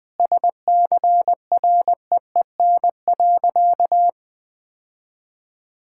Morse code, heard as a keyed tone, about 20 wpm, 700 Hz. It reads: SCREEN.
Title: Sentences with 6-Word Phrases - Spoken Beginning - 20wpm